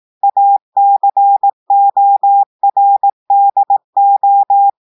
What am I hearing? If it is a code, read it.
Morse code, heard as a keyed tone, about 18 wpm, 800 Hz. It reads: ACORDO
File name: audios / portugais / portugais texte en clair 2 vitesse 18 mots minutes